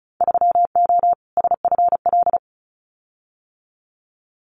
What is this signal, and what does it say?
Morse code, heard as a keyed tone, about 34 wpm, 700 Hz. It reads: 3O SFL
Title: Winter Field Day Exchanges 35wpm